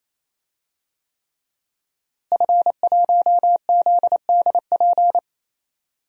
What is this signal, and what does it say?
Morse code, heard as a keyed tone, about 27 wpm, 700 Hz. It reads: F1ZDP
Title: DXCC-100 Call Signs - 5 Characters - Repeated 3x - 28wpm